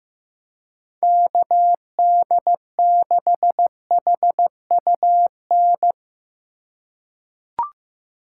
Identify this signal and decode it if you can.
Morse code, heard as a keyed tone, about 15 wpm, 700 Hz. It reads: KD6HUN
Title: USA Call Signs - 6x Character Spacing - 15wpm